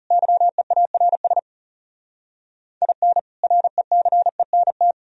Morse code, heard as a keyed tone, about 37 wpm, 700 Hz. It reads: YEARS IN RECENT
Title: Speed-Racing - Sets of 3 Words 30wpm